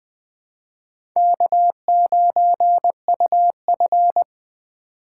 Morse code, heard as a keyed tone, about 20 wpm, 700 Hz. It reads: K9UF
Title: USA Call Signs - 4 Characters - 2x Character Spacing - 20wpm